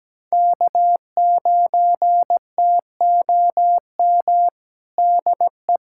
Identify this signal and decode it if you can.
Morse code, heard as a keyed tone, about 17 wpm, 700 Hz. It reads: K9TOM DE
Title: USA Callsign DE Callsign 17wpm